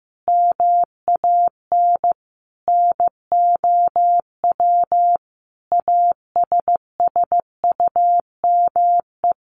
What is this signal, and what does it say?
Morse code, heard as a keyed tone, about 15 wpm, 700 Hz. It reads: MAN NOW ASSUME
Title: Sentences from Top 1500 Words 15wpm